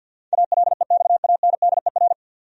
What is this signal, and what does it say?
Morse code, heard as a keyed tone, about 49 wpm, 700 Hz. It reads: ALEXANDER
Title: USA Common Names - 50wpm